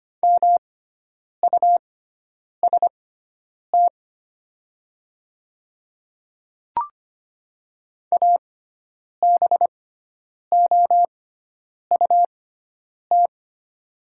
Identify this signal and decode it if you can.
Morse code, heard as a keyed tone, about 25 wpm, spaced as for 9 wpm, 690 Hz. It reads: MUST ABOUT
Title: 6x Character Spacing - Top 200 Words - 25wpm